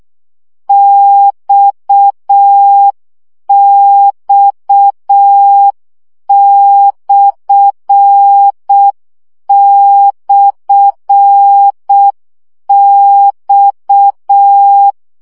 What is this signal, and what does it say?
Morse code, heard as a keyed tone, about 6 wpm, 800 Hz. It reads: XX//X
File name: CW gyakorlas x per 5 baud